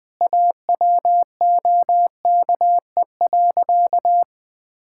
Morse code, heard as a keyed tone, about 20 wpm, 700 Hz. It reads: AWOKE.
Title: Sentences with 4-Word Phrases - Spoken Beginning - 20wpm